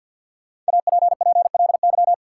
Morse code, heard as a keyed tone, about 53 wpm, 700 Hz. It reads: APPLY
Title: Sentences from Top 1200 Words - New Word Drill Alt - 50wpm